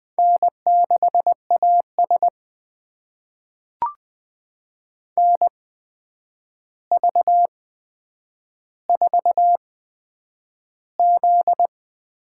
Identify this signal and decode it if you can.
Morse code, heard as a keyed tone, about 20 wpm, spaced as for 9 wpm, 700 Hz. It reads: N6AS NV4Z
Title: USA Call Signs - 4 Characters - 8x Character Spacing - 20wpm